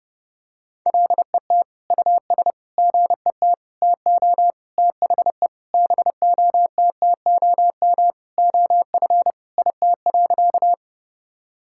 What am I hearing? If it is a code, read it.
Morse code, heard as a keyed tone, about 30 wpm, 700 Hz. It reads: LET US GET TO THE BOTTOM OF IT.
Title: Sentences from Top 1000 Words - Review Part 6 - 30wpm